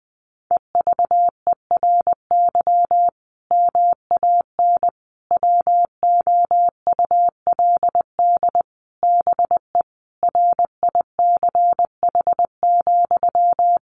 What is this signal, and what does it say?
Morse code, heard as a keyed tone, about 19 wpm, 700 Hz. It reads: EVERY MAN WOULD BE RICH,
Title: Sentences from Top 1500 Words 20wpm